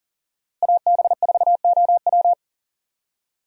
Morse code, heard as a keyed tone, about 39 wpm, 700 Hz. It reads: AB4OW
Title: USA Call Signs - 5 Characters - Repeated 3x - 40wpm